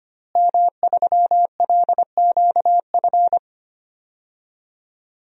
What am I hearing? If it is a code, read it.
Morse code, heard as a keyed tone, about 25 wpm, 700 Hz. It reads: M3LQF
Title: DXCC-100 Call Signs - 5 Characters - Repeated 2x - 25wpm